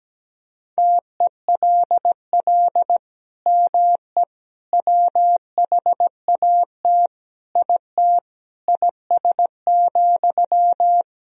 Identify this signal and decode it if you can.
Morse code, heard as a keyed tone, about 17 wpm, 700 Hz. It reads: TELL ME WHAT IT IS,
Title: Sentences from Top 1000 Words - Review Part 1 - 17wpm